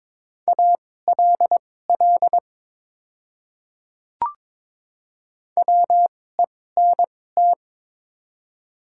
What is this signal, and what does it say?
Morse code, heard as a keyed tone, about 22 wpm, spaced as for 16 wpm, 700 Hz. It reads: ALL WENT
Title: 2x Character Spacing - Top 300 Words - 22wpm